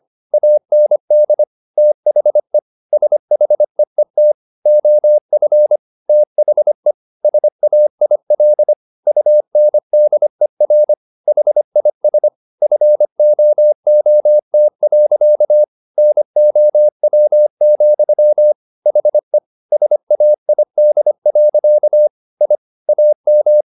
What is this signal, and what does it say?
Morse code, heard as a keyed tone, about 25 wpm, 600 Hz. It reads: AND THE SHEET OF THE SAIL UNDER HIS FOOT. NOW, HE SAID. I AM